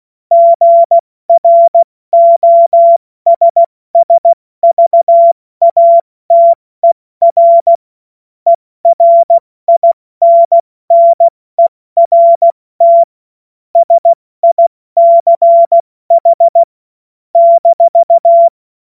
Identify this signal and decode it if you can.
Morse code, heard as a keyed tone, about 16 wpm, spaced as for 14 wpm, 680 Hz. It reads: GROSSVATER ERINNERT SICH -
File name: Morsekurs / Witze / Witz 19. Oktober 2021, 70 F80